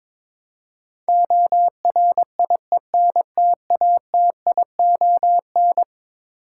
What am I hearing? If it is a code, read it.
Morse code, heard as a keyed tone, about 22 wpm, 700 Hz. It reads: ORIENTATION
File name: practice / 22wpm / Eleven Letter Words - 22wpm